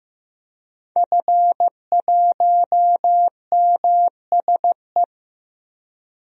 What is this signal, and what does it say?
Morse code, heard as a keyed tone, about 15 wpm, 700 Hz. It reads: F1MSE
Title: DXCC-100 Call Signs - 5 Characters - 15wpm